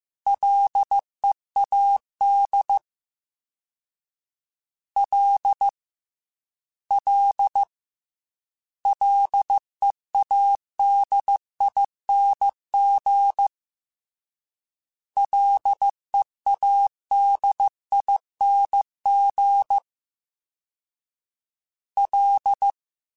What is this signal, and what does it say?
Morse code, heard as a keyed tone, about 15 wpm, 800 Hz. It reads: LEAD L L LEADING LEADING L